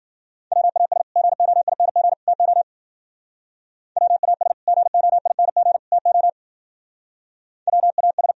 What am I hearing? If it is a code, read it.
Morse code, heard as a keyed tone, about 59 wpm, 700 Hz. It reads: WAS GOING TO WAS GOING TO WAS